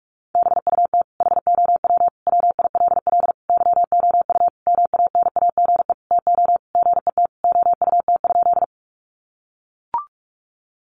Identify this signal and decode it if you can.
Morse code, heard as a keyed tone, about 44 wpm, 700 Hz. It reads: BUT HOW WILL YOU MANAGE TO GET OUT?
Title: Sentences from Top 1400 Words 45wpm